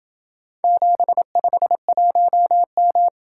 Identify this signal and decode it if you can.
Morse code, heard as a keyed tone, about 27 wpm, 700 Hz. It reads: 751M